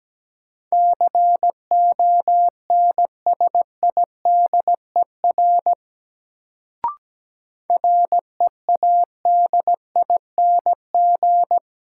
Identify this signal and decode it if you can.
Morse code, heard as a keyed tone, about 17 wpm, 700 Hz. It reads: CONSIDER READING
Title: Sentences from Top 600 Words - New Word Drill - 17wpm